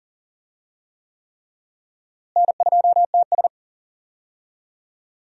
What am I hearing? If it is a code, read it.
Morse code, heard as a keyed tone, about 40 wpm, 700 Hz. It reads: N2TS
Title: USA Call Signs - 4 Characters - Repeated 3x - 40wpm